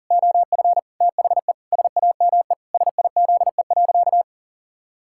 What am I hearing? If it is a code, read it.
Morse code, heard as a keyed tone, about 40 wpm, 700 Hz. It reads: OF THE SAME SIZE.
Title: Pangram Challenge v2 - 40wpm